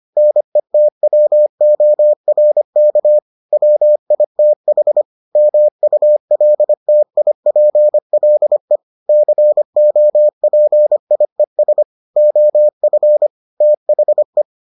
Morse code, heard as a keyed tone, about 25 wpm, 600 Hz. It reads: NETWORK WITH MULTIPLE COPIES OF THE